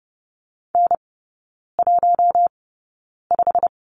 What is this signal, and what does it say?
Morse code, heard as a keyed tone, about 30 wpm, spaced as for 9 wpm, 690 Hz. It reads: N15
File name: Lesson 16e - Letter c - Three Characters 30wpm